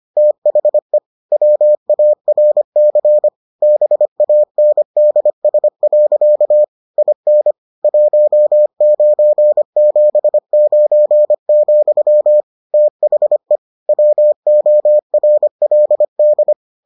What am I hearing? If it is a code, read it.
Morse code, heard as a keyed tone, about 25 wpm, 600 Hz. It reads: THE WARC BANDS. IN 1979, THE WORLD